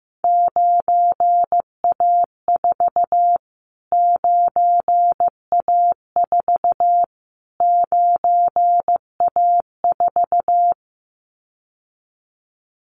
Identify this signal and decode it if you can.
Morse code, heard as a keyed tone, about 15 wpm, 700 Hz. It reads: 9A4 9A4 9A4
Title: Callsign Prefixes for Top 25 DXCC - Repeated 3x - 15wpm